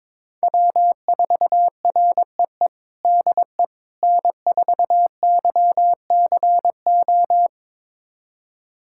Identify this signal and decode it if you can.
Morse code, heard as a keyed tone, about 22 wpm, 700 Hz. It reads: W4REE DE N4YCO
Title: USA Callsign DE Callsign - 5 Characters - 22wpm